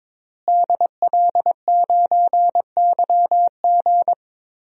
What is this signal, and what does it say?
Morse code, heard as a keyed tone, about 22 wpm, 700 Hz. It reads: DL9YG
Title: DXCC-100 Call Signs - 5 Characters - Repeated 2x - 22wpm